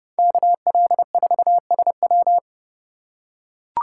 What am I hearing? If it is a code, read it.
Morse code, heard as a keyed tone, about 29 wpm, 700 Hz. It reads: KL4SW